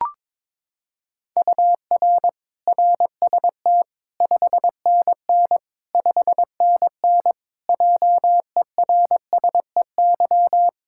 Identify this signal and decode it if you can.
Morse code, heard as a keyed tone, about 21 wpm, 700 Hz. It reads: UR RST 5NN 5NN JERSEY
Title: Practice QSO - Exchange 1 - Beginning - 22wpm